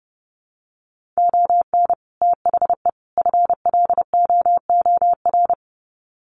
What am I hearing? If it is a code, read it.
Morse code, heard as a keyed tone, about 30 wpm, 700 Hz. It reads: ON THE FLOOR
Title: Sets of 3 Words 30wpm